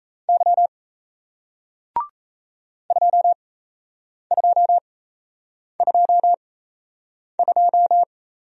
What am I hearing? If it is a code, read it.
Morse code, heard as a keyed tone, about 37 wpm, spaced as for 8 wpm, 700 Hz. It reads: Y 2222